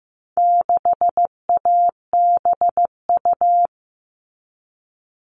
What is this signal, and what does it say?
Morse code, heard as a keyed tone, about 15 wpm, 700 Hz. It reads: 6ABU
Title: All Cut Numbers - 4-Digit 15wpm